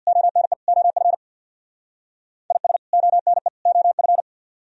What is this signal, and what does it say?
Morse code, heard as a keyed tone, about 63 wpm, 700 Hz. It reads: ONE OF IS ONE OF